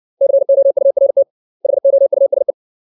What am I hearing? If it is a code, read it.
Morse code, heard as a keyed tone, about 60 wpm, 550 Hz. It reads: COURT HOUSE
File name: 632 words / 632 Words 60WPM